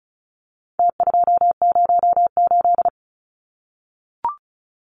Morse code, heard as a keyed tone, about 34 wpm, 700 Hz. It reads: T208
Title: Numbers - Contest Style - 4-Digit 35wpm